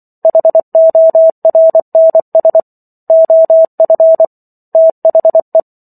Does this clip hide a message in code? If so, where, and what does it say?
Yes: Morse code from 0.2 to 5.6 s, heard as a keyed tone, about 24 wpm, 650 Hz. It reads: HORNS OF THE